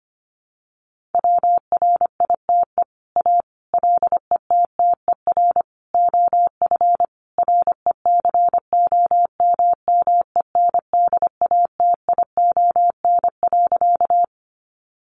Morse code, heard as a keyed tone, about 25 wpm, 700 Hz. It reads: WRITE A LETTER OF RECOMMENDATION.
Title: Sentences from Top 1400 Words 25wpm